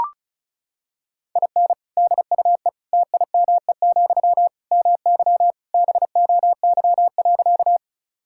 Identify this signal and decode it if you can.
Morse code, heard as a keyed tone, about 34 wpm, 700 Hz. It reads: IN DUE TIME, MY BOY.
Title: Sentences from Top 1000 Words - Review Part 6 - 35wpm